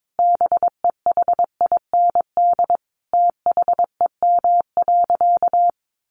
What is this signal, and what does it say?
Morse code, heard as a keyed tone, about 22 wpm, 700 Hz. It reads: BEHIND THEM.
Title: Sentences from Top 1400 Words 22wpm